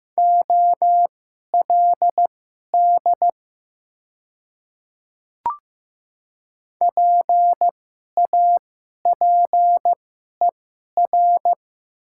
Morse code, heard as a keyed tone, about 15 wpm, spaced as for 11 wpm, 700 Hz. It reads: OLD PAPER
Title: 2x Character Spacing - Top 300 Words - 15wpm